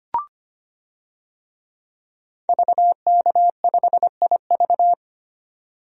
Morse code, heard as a keyed tone, about 25 wpm, 700 Hz. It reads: VK5IV
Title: DXCC-100 Call Signs - 5 Characters - 25wpm